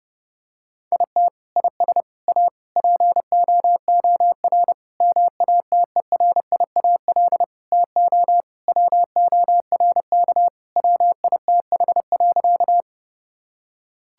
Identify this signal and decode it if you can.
Morse code, heard as a keyed tone, about 32 wpm, 700 Hz. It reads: IT IS A POOR MATERIAL TO WORK WITH.
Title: Sentences from Top 1000 Words - Review Part 5 - 30wpm